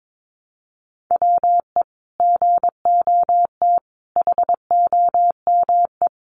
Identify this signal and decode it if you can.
Morse code, heard as a keyed tone, about 22 wpm, 700 Hz. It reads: WE GOT HOME
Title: Sentences from Top 900 Words 22wpm